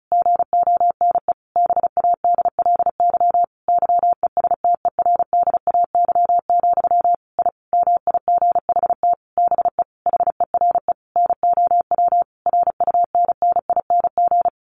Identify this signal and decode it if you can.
Morse code, heard as a keyed tone, about 34 wpm, 700 Hz. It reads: GONE BADLY YESTERDAY, I MIGHT BE HERE NOW RUNNING
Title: A Sound Of Thunder 35wpm